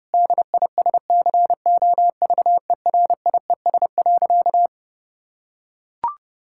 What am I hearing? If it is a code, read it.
Morse code, heard as a keyed tone, about 32 wpm, 700 Hz. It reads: DISCOVERIES.
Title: Pangram Challenge v2 - 30wpm